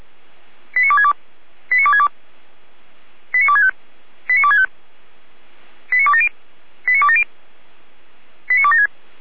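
Five-tone selective call (ZVEI), heard as quick runs of five tones, tones of about 70 ms. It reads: 89262 89262 89265 89265 89270 89270 89276
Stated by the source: ZVEI five-tone selective call received on SDR.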